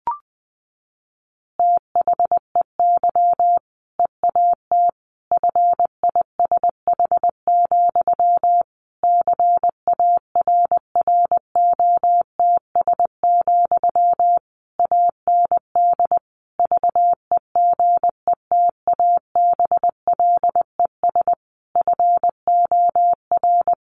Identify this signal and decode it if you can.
Morse code, heard as a keyed tone, about 20 wpm, 700 Hz. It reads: THEY EAT FISH, CARROTS, AND VEGETABLES FOR
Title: Easy Phrases 20wpm